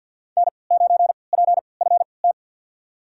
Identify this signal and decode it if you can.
Morse code, heard as a keyed tone, about 53 wpm, spaced as for 29 wpm, 700 Hz. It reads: N9PFT
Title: USA Call Signs - 5 Characters - 3x Character Spacing - 50wpm